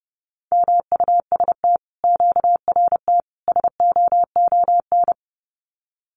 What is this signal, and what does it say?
Morse code, heard as a keyed tone, about 30 wpm, 700 Hz. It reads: MUST QRT SOON
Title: Common QSO Phrases Encore 30wpm